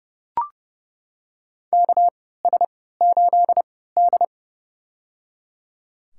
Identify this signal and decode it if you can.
Morse code, heard as a keyed tone, about 30 wpm, spaced as for 17 wpm, 700 Hz. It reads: KS8D
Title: USA Call Signs - 3x Character Spacing - 30wpm